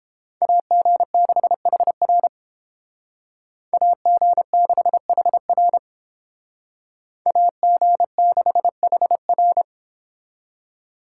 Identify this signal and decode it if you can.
Morse code, heard as a keyed tone, about 30 wpm, 700 Hz. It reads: AG6HR AG6HR AG6HR